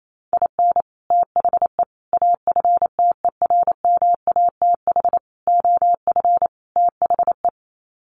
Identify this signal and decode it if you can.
Morse code, heard as a keyed tone, about 28 wpm, 700 Hz. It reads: IN THE AFTERMATH OF THE
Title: Sets of 5 Words 28wpm